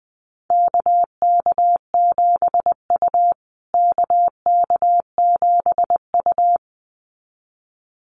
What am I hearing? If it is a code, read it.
Morse code, heard as a keyed tone, about 20 wpm, 700 Hz. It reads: KK7U KK7U
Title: USA Call Signs Repeated 2x 20wpm